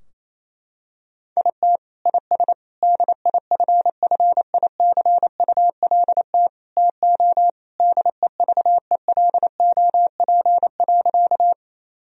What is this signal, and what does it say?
Morse code, heard as a keyed tone, about 28 wpm, 700 Hz. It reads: IT IS DIFFICULT TO DEVELOP.
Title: Sentences from Top 1000 Words - Review Part 2 - 28wpm